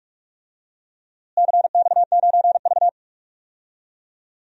Morse code, heard as a keyed tone, about 45 wpm, 700 Hz. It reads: CX9V
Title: DXCC-100 Call Signs - 4 Characters - Repeated 3x - 45wpm